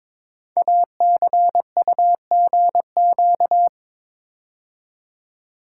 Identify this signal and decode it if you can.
Morse code, heard as a keyed tone, about 22 wpm, 690 Hz. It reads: ACUGQ
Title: Code Groups - 5 Letters - 22wpm